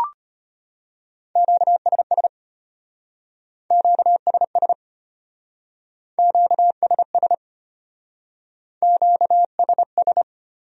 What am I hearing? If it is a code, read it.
Morse code, heard as a keyed tone, about 34 wpm, 700 Hz. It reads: QSS QSS QSS QSS